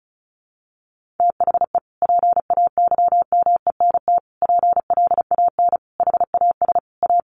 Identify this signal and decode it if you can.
Morse code, heard as a keyed tone, about 34 wpm, 700 Hz. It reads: THE PAYMENT PLAN HAS A
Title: Sentences from Top 1500 Words 35wpm